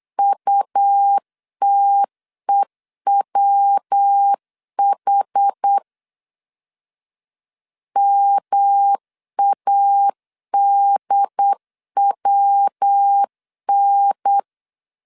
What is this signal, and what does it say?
Morse code, heard as a keyed tone, about 8 wpm, 800 Hz. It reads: UTEWH MADWN